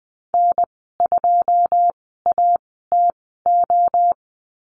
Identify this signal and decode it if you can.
Morse code, heard as a keyed tone, about 20 wpm, spaced as for 14 wpm, 700 Hz. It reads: N2ATO
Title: 161 Lesson 26g - Letter g - Call Signs 20wpm